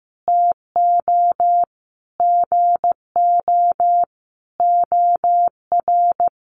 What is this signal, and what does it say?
Morse code, heard as a keyed tone, about 15 wpm, 700 Hz. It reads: TO GO OR
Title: Sentences from Top 1400 Words 15wpm